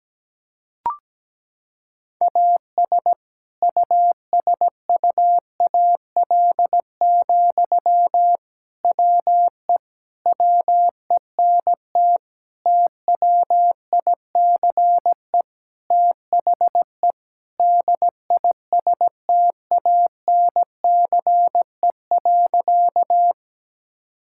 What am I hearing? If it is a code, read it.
Morse code, heard as a keyed tone, about 17 wpm, 700 Hz. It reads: AS USUAL, WE WENT TWICE THE DISTANCE.
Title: Sentences from Top 1600 Words 17wpm